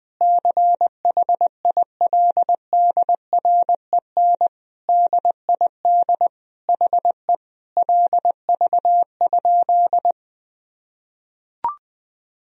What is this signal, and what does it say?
Morse code, heard as a keyed tone, about 20 wpm, 700 Hz. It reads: CHILDREN DID HE LV?
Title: Sentences from Top 400 Words and Abbreviations 20wpm